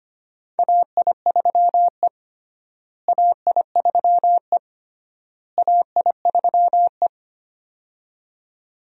Morse code, heard as a keyed tone, about 25 wpm, 700 Hz. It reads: AI3E AI3E AI3E
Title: USA Call Signs - 4 Characters - Repeated 3x - 25wpm